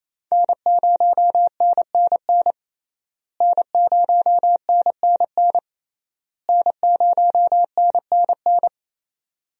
Morse code, heard as a keyed tone, about 27 wpm, 700 Hz. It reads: N0NNN N0NNN N0NNN
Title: USA Call Signs - 5 Characters - Repeated 3x - 28wpm